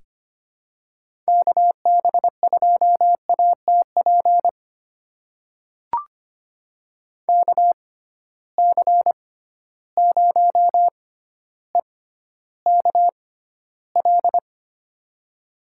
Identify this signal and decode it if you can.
Morse code, heard as a keyed tone, about 25 wpm, spaced as for 13 wpm, 700 Hz. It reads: KB2ATP KC0EKL
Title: USA Call Signs - 6 Characters - 6x Character Spacing - 25wpm